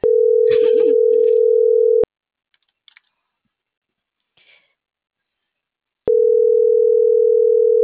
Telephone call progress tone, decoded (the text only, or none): ringback tone